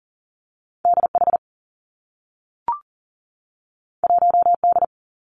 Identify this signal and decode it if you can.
Morse code, heard as a keyed tone, about 40 wpm, 700 Hz. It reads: D H 1 D
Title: Two Challenging Letter-Number Combinations 40wpm